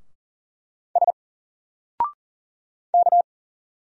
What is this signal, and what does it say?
Morse code, heard as a keyed tone, about 40 wpm, 700 Hz. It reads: S K
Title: Single Letter-Number Alt 40wpm